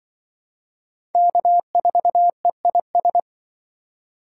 Morse code, heard as a keyed tone, about 24 wpm, 700 Hz. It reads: K4EIS
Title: Speed Racing - USA 1x3 Call Signs 20wpm